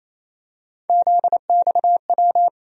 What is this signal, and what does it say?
Morse code, heard as a keyed tone, about 28 wpm, 700 Hz. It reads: ZXW
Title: USA Call Sign Postfixes - 3 Letters 28wpm